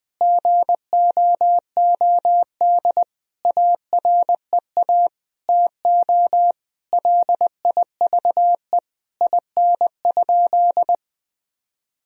Morse code, heard as a keyed tone, about 20 wpm, 700 Hz. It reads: GOOD AREA TO LIVE IN?
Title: Sentences from Top 1000 Words - Review Part 2 - 20wpm